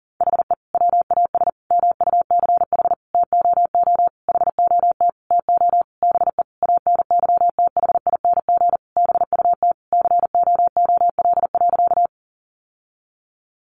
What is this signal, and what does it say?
Morse code, heard as a keyed tone, about 40 wpm, 700 Hz. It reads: HE WAS MUCH TOO HOT TO BE ANYTHING BUT COOL.
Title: Sentences from Top 1200 Words 40wpm